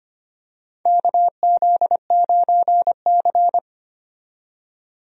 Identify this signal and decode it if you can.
Morse code, heard as a keyed tone, about 25 wpm, 700 Hz. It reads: KZ9C
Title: USA Call Signs - 4 Characters - 25wpm